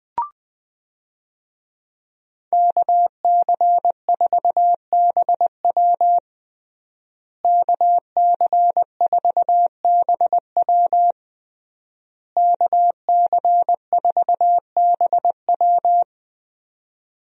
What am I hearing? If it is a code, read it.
Morse code, heard as a keyed tone, about 20 wpm, 700 Hz. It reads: KC4BW KC4BW KC4BW
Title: USA Call Signs - 5 Characters - Repeated 3x - 20wpm